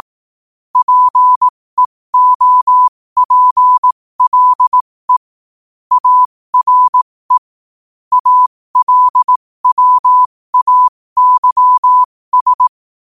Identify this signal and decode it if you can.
Morse code, heard as a keyed tone, about 18 wpm, spaced as for 15 wpm, 1000 Hz. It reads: PEOPLE ARE ALWAYS
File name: qotdfiles / 15wpm